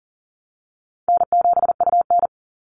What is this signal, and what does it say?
Morse code, heard as a keyed tone, about 38 wpm, 700 Hz. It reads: N7UN
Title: NA SOTA Activator Callsigns 40wpm